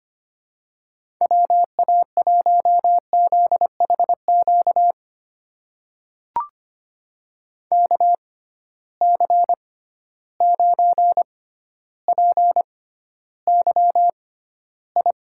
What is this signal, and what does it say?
Morse code, heard as a keyed tone, about 25 wpm, spaced as for 13 wpm, 700 Hz. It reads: WA1ZHQ KC9PYI
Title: USA Call Signs - 6 Characters - 6x Character Spacing - 25wpm